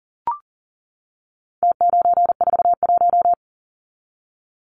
Morse code, heard as a keyed tone, about 40 wpm, 700 Hz. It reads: T941
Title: Numbers - Contest Style - 4-Digit 40wpm